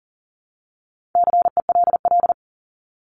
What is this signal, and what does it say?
Morse code, heard as a keyed tone, about 40 wpm, 700 Hz. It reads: CELL
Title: Top 1001-1250 Words 40wpm